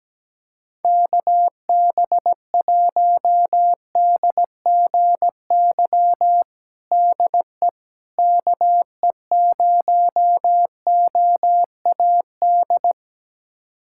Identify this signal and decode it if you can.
Morse code, heard as a keyed tone, about 17 wpm, 700 Hz. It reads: KB1DGY DE KE0OAD
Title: USA Callsign DE Callsign - 6 Characters - 17wpm